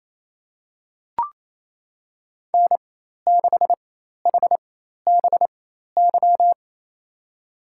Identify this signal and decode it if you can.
Morse code, heard as a keyed tone, about 28 wpm, spaced as for 13 wpm, 700 Hz. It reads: N6HBY
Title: USA Call Signs - 4x Character Spacing - 28wpm